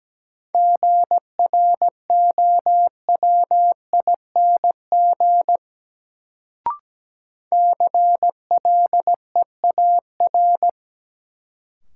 Morse code, heard as a keyed tone, about 17 wpm, 700 Hz. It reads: GROWING CLEAR
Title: Sets of 5 Words Encore - New Word Drill - 17wpm